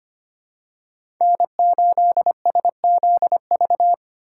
Morse code, heard as a keyed tone, about 25 wpm, 700 Hz. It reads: N8SZV
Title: USA Call Signs - 5 Characters - 2x Character Spacing - 25wpm